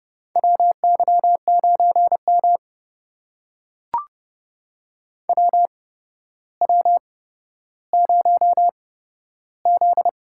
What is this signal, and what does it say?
Morse code, heard as a keyed tone, about 28 wpm, spaced as for 13 wpm, 700 Hz. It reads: WY9M WW0Z
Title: USA Call Signs - 4 Characters - 8x Character Spacing - 30wpm